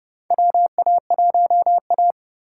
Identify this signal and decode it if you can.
Morse code, heard as a keyed tone, about 30 wpm, 700 Hz. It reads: WA1A
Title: USA Call Signs - 4 Characters - 6x Character Spacing - 30wpm